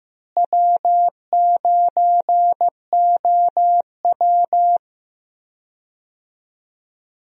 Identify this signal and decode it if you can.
Morse code, heard as a keyed tone, about 15 wpm, 700 Hz. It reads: W9OW
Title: USA Call Signs - 4 Characters - Repeated 2x - 15wpm